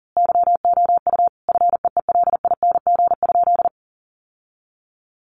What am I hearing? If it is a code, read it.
Morse code, heard as a keyed tone, about 40 wpm, 700 Hz. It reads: YOU FEELING?
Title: Sentences from Top 1200 Words 40wpm